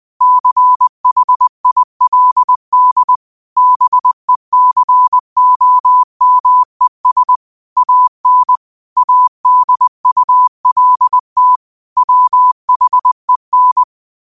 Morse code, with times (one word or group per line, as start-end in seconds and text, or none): 0.2-3.2 CHILD
3.6-7.4 BECOMES
7.8-8.6 AN
9.0-11.6 ADULT
12.0-13.8 WHEN